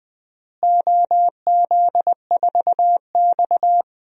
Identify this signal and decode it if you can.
Morse code, heard as a keyed tone, about 20 wpm, 700 Hz. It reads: OZ4X
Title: DXCC-100 Call Signs - 4 Characters - Repeated 3x - 20wpm